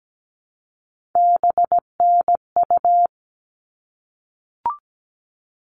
Morse code, heard as a keyed tone, about 17 wpm, 700 Hz. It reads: BNU
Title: All Cut Numbers - 3-Digit 17wpm